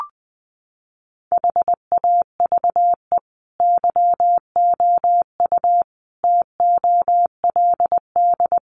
Morse code, heard as a keyed tone, about 20 wpm, 700 Hz. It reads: HAVE YOU TOLD
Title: Sentences from Top 500 Words 20wpm